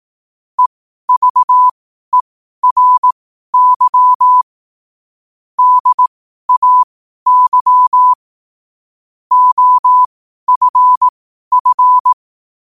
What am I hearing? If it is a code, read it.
Morse code, heard as a keyed tone, about 18 wpm, spaced as for 12 wpm, 1000 Hz. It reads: EVERY DAY OFF
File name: qotdfiles / 12wpm